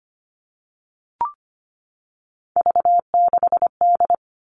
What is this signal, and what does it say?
Morse code, heard as a keyed tone, about 25 wpm, 700 Hz. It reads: V6D
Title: All Cut Numbers - 3-Digit 25wpm